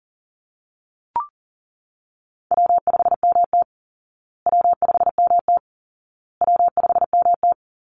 Morse code, heard as a keyed tone, about 40 wpm, 700 Hz. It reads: W5MT W5MT W5MT
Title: Easy Call Signs Repeated 3x - 40wpm